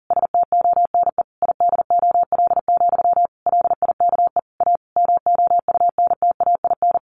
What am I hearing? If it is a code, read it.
Morse code, heard as a keyed tone, about 37 wpm, 700 Hz. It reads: STONE IDOL, LIKE A MOUNTAIN
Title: A Sound Of Thunder 40wpm